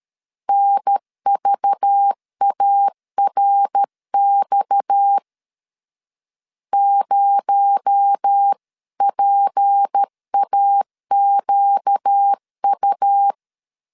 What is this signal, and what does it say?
Morse code, heard as a keyed tone, about 13 wpm, 800 Hz. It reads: NVARX 0PAQU
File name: CW / DET / 1 6 1500C S B 12wpm 3431